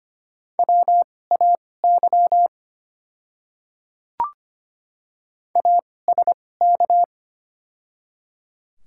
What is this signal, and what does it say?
Morse code, heard as a keyed tone, about 25 wpm, spaced as for 18 wpm, 690 Hz. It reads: WAY ASK
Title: 2x Character Spacing - Top 300 Words - 25wpm